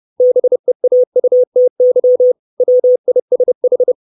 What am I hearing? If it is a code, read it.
Morse code, heard as a keyed tone, about 30 wpm, 500 Hz. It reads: BEAUTY WISH
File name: 632 Words 30WPM